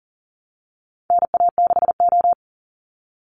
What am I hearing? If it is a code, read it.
Morse code, heard as a keyed tone, about 40 wpm, 700 Hz. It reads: NA6O
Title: NA SOTA Chaser Callsigns 40wpm